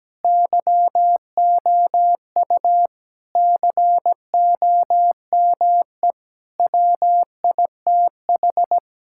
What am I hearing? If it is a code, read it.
Morse code, heard as a keyed tone, about 17 wpm, 700 Hz. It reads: YOU COME WITH
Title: Sentences from Top 1000 Words - Review Part 6 - 17wpm